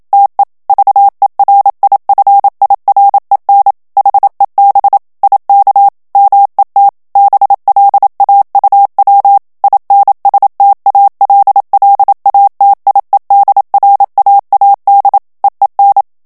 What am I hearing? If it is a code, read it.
Morse code, heard as a keyed tone, about 27 wpm, 800 Hz. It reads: TE VERIFIREN HEB IK MET BLAUW INSTALLATIEDRAAD EEN
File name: HOLLANDAIS texte clair 1 vitesse 26 mots minute